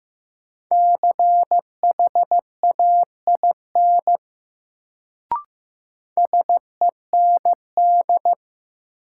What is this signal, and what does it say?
Morse code, heard as a keyed tone, about 15 wpm, 700 Hz. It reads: CHAIN SEND
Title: Sentences from Top 1500 Words - New Word Drill - 15wpm